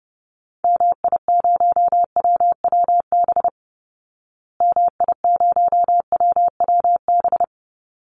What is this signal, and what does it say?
Morse code, heard as a keyed tone, about 30 wpm, 700 Hz. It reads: MI0WWB MI0WWB
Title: Call Signs Repeated 3x - 30wpm